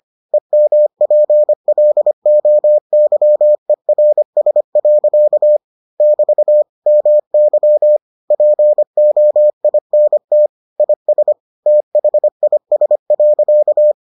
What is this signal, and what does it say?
Morse code, heard as a keyed tone, about 25 wpm, 600 Hz. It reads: EMPLOYERS. <BT> MY POINT IS THIS.